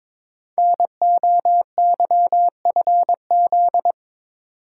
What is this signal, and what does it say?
Morse code, heard as a keyed tone, about 22 wpm, 700 Hz. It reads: NOYFZ
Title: Code Groups - 5 Letters - 22wpm